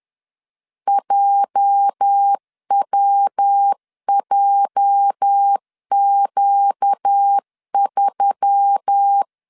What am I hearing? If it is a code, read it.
Morse code, heard as a keyed tone, about 10 wpm, 800 Hz. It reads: JWJQ3
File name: LEZIONI / 8 YJ45 / 8 YJ 10wpm 2651